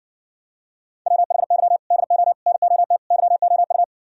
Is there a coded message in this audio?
Yes: Morse code from 1.1 to 3.8 s, heard as a keyed tone, about 60 wpm, 700 Hz. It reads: WHY DO NOT YOU